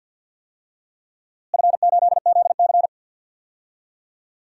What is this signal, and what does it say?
Morse code, heard as a keyed tone, about 47 wpm, 700 Hz. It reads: F8ZC
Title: DXCC-100 Call Signs - 4 Characters - Repeated 3x - 50wpm